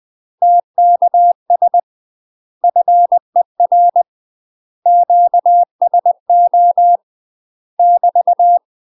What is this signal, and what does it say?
Morse code, heard as a keyed tone, about 20 wpm, 700 Hz. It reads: TKS FER QSO <BT>
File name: UK MASTER QSOs TRAINING / 20WPM / UK MASTER QSOs 0047